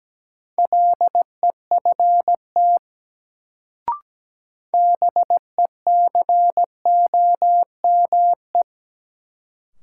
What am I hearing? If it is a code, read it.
Morse code, heard as a keyed tone, about 17 wpm, 700 Hz. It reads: LEFT BECOME
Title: Sets of 5 Words Encore - New Word Drill - 17wpm